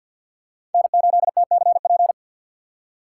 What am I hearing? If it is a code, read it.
Morse code, heard as a keyed tone, about 52 wpm, 700 Hz. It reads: N8TCP
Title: USA Call Signs - 5 Characters - 4x Character Spacing - 50wpm